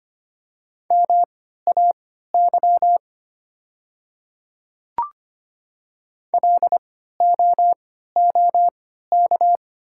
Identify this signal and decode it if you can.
Morse code, heard as a keyed tone, about 25 wpm, spaced as for 14 wpm, 700 Hz. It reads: MAY LOOK
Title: 3x Character Spacing - Top 100 Words - 25wpm